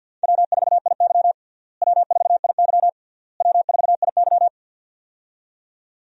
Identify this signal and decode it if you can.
Morse code, heard as a keyed tone, about 53 wpm, 700 Hz. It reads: W4IY W4IY W4IY
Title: USA Call Signs - 4 Characters - Repeated 3x - 50wpm